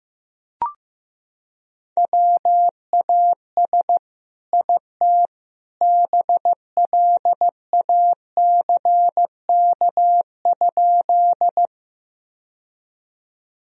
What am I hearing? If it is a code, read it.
Morse code, heard as a keyed tone, about 16 wpm, 700 Hz. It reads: WAS IT BLACK?
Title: Sentences from Top 1000 Words - Review Part 2 - 15wpm